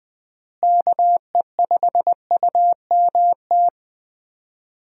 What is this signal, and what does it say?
Morse code, heard as a keyed tone, about 20 wpm, 700 Hz. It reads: KE5UMT
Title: USA Call Signs - 6 Characters - Repeated 2x - 20wpm